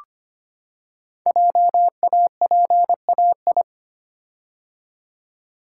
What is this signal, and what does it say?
Morse code, heard as a keyed tone, about 26 wpm, 700 Hz. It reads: JAPAI
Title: Code Groups - 5 Letter:Numbers - 25wpm